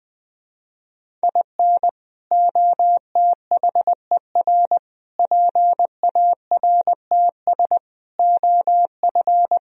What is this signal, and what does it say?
Morse code, heard as a keyed tone, about 21 wpm, 700 Hz. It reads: IN OTHER PARTS OF